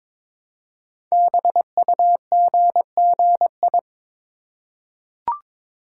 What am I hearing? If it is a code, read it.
Morse code, heard as a keyed tone, about 22 wpm, 700 Hz. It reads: BUGGI
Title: Code Groups - 5 Letter:Numbers - 22wpm